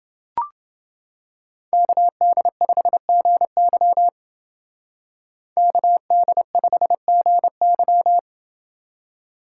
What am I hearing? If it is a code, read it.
Morse code, heard as a keyed tone, about 28 wpm, 700 Hz. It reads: KD5GY KD5GY